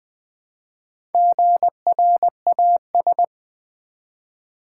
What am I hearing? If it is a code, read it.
Morse code, heard as a keyed tone, about 20 wpm, 700 Hz. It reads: GRAS
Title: Build A Word - 1 Letter At A Time - 3 to 5-Letter Words - 20wpm